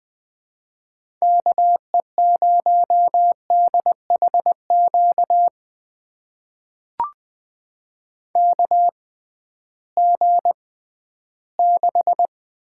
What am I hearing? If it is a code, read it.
Morse code, heard as a keyed tone, about 20 wpm, spaced as for 13 wpm, 700 Hz. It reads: KE0DHQ KG6
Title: USA Call Signs - 6 Characters - 6x Character Spacing - 20wpm